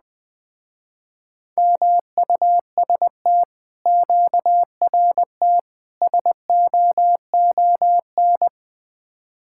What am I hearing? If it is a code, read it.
Morse code, heard as a keyed tone, about 20 wpm, 700 Hz. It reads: MUST QRT SOON
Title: Speed-Racing Common QSO Phrases - 17wpm